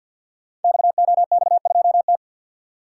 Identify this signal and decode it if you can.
Morse code, heard as a keyed tone, about 52 wpm, 700 Hz. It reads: XOX2T
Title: Code Groups - 5 Letter:Numbers - 50wpm